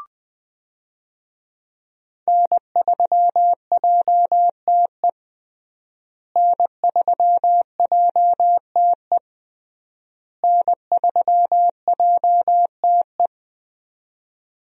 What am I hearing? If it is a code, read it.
Morse code, heard as a keyed tone, about 20 wpm, 700 Hz. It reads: N3JTE N3JTE N3JTE
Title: USA Call Signs - 5 Characters - Repeated 3x - 20wpm